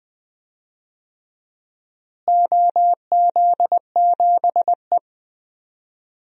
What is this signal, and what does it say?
Morse code, heard as a keyed tone, about 20 wpm, 700 Hz. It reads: OZ7E
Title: DXCC-100 Call Signs - 4 Characters - Repeated 3x - 20wpm